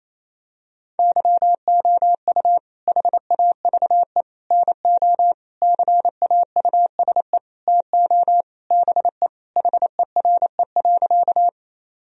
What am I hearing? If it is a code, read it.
Morse code, heard as a keyed tone, about 28 wpm, 700 Hz. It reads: YOU HAVE NO CAUSE TO BE HERE.
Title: Sentences from Top 1000 Words - Review Part 1 - 28wpm